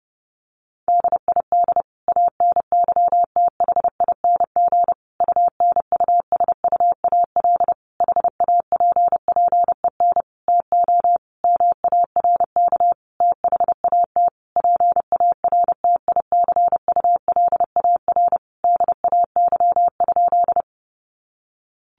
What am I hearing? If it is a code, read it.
Morse code, heard as a keyed tone, about 30 wpm, 700 Hz. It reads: DID ANYTHING UNUSUAL HAPPEN TO MARK THAT PARTICULAR DAY?
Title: Sentences from Top 1400 Words 30wpm